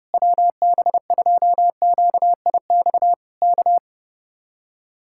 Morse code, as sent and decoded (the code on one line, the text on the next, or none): .-- -... ..--- --.- .. -..- / -.-
WB2QIX K